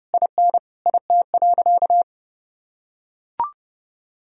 Morse code, heard as a keyed tone, about 30 wpm, 700 Hz. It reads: IN IT.
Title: Sentences from Top 1000 Words - Review Part 2 - 30wpm